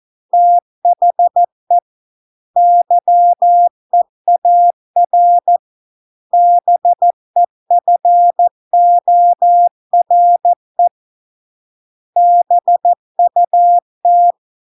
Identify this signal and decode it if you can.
Morse code, heard as a keyed tone, about 14 wpm, 700 Hz. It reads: THE YEAR BEFORE BUT